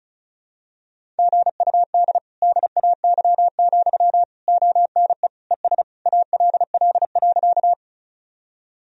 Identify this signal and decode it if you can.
Morse code, heard as a keyed tone, about 34 wpm, 700 Hz. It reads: GUD DAY, ONE ES ALL.
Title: Sentences from Top 200 Words and Abbreviations 35wpm